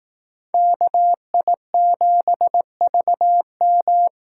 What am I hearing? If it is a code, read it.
Morse code, heard as a keyed tone, about 18 wpm, 700 Hz. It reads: KI7VM
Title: Speed Racing - USA Call Signs 15wpm